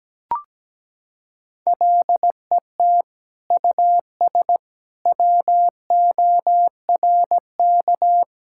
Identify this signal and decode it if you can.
Morse code, heard as a keyed tone, about 17 wpm, 700 Hz. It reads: LET US WORK
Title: Sentences from Top 1000 Words - Review Part 5 - 17wpm